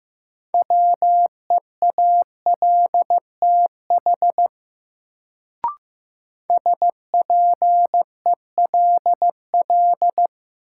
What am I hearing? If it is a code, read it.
Morse code, heard as a keyed tone, about 15 wpm, 700 Hz. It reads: WEALTH SPELL
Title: Sentences from Top 1200 Words - New Word Drill Alt - 15wpm